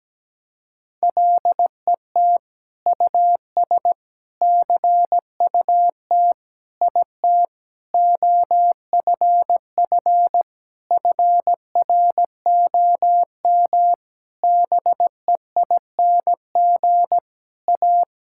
Morse code, heard as a keyed tone, about 17 wpm, 700 Hz. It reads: LET US CUT IT OFF FROM BEING A
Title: Sentences from Top 1000 Words - Review Part 6 - 17wpm